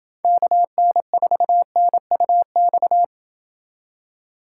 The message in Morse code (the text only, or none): KN4NUX